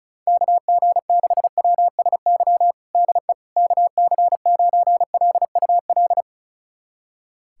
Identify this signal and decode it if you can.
Morse code, heard as a keyed tone, about 34 wpm, 700 Hz. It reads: KG6WSY DE KC9LUL
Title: USA Callsign DE Callsign - 6 Characters - 35wpm